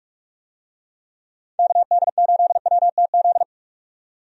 Morse code, heard as a keyed tone, about 46 wpm, 700 Hz. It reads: KD8WTZ
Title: USA Call Signs - 6 Characters - 45wpm